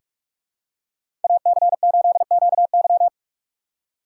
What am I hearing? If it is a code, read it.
Morse code, heard as a keyed tone, about 44 wpm, 690 Hz. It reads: AC8QY
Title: USA Call Signs - 5 Characters - 6x Character Spacing - 45wpm